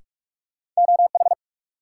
Morse code, heard as a keyed tone, about 44 wpm, 700 Hz. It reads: OH
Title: Sentences from Top 500 Words - New Word Drill - 45wpm